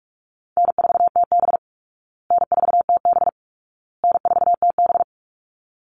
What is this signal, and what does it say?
Morse code, heard as a keyed tone, about 44 wpm, 700 Hz. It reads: N4TB N4TB N4TB
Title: Call Signs Repeated 2x - 45wpm